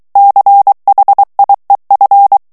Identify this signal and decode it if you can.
Morse code, heard as a keyed tone, about 23 wpm, 800 Hz. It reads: CHIEF
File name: ANGLAIS texte clair 3 vitesse 22 mots minute